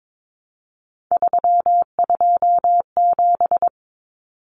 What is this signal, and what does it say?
Morse code, heard as a keyed tone, about 22 wpm, 700 Hz. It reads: 327
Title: Common Cut Numbers - 3-Digit 22wpm